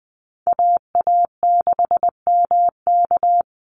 Morse code, heard as a keyed tone, about 20 wpm, 700 Hz. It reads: AA6MK
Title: Call Signs Repeated 2x - 20wpm